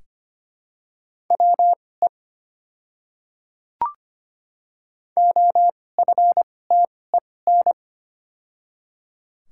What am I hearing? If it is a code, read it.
Morse code, heard as a keyed tone, about 25 wpm, spaced as for 18 wpm, 700 Hz. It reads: WE OFTEN
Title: 2x Character Spacing - Top 300 Words - 25wpm